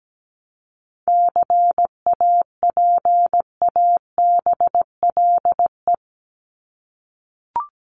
Morse code, heard as a keyed tone, about 17 wpm, 700 Hz. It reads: CAPABLE
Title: Top 1001-1250 Words 17wpm